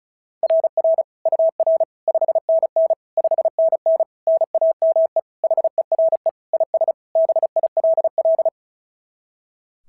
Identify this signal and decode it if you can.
Morse code, heard as a keyed tone, about 36 wpm, 690 Hz. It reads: RR UR 5NN 5NN NAME HERE IS BILL
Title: SOTA QSO Practice 35wpm